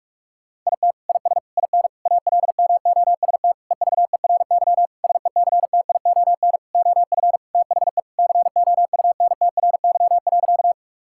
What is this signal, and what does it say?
Morse code, heard as a keyed tone, about 46 wpm, 700 Hz. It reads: IT IS IN ALMOST EVERY SECTION OF THE COUNTRY.
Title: Sentences from Top 1000 Words - Review Part 5 - 45wpm